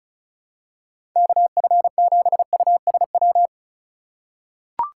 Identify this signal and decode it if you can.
Morse code, heard as a keyed tone, about 34 wpm, 700 Hz. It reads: KF7USW
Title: USA Call Signs - 6 Characters - 2x Character Spacing - 35wpm